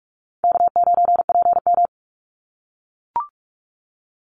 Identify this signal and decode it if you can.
Morse code, heard as a keyed tone, about 44 wpm, 700 Hz. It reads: K9PM
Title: NA SOTA Chaser Callsigns 45wpm